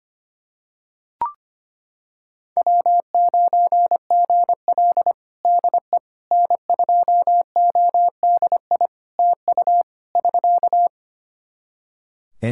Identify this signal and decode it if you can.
Morse code, heard as a keyed tone, about 25 wpm, 700 Hz. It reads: W9GL DE N2ODI TU <SK>
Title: Practice QSO - Exchange 4 - Ending - 25wpm